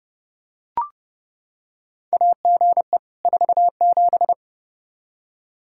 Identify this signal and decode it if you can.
Morse code, heard as a keyed tone, about 30 wpm, 700 Hz. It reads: AGE 47
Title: Practice QSO - Exchange 3 - More Info - 30wpm